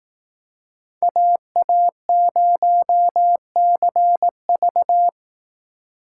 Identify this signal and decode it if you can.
Morse code, heard as a keyed tone, about 18 wpm, 700 Hz. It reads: AA0CV
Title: Speed Racing - USA 2x2 Call Signs 15wpm